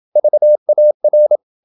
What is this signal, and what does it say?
Morse code, heard as a keyed tone, about 27 wpm, 600 Hz. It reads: VAR
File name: Ferie QRQ 27wpm